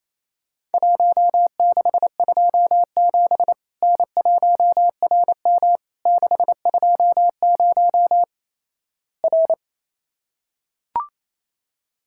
Morse code, heard as a keyed tone, about 28 wpm, 700 Hz. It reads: 1627 N1RM 620 R
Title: Worked All Europe DX Contest - QTC Exchange - 28wpm